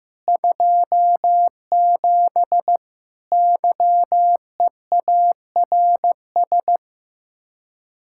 Morse code, heard as a keyed tone, about 15 wpm, 700 Hz. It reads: 27 YEARS
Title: Speed-Racing Common QSO Phrases Encore - 15wpm